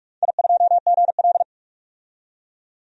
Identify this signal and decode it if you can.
Morse code, heard as a keyed tone, about 44 wpm, 700 Hz. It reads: I2GL
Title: DXCC-100 Call Signs - 4 Characters - 45wpm